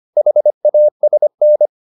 Morse code, heard as a keyed tone, about 25 wpm, 600 Hz. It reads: HASN
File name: the old man and the sea part 1 25wpm